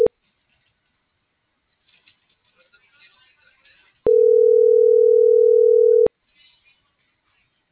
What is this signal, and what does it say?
Ringback tone (North America), 440 + 480 Hz.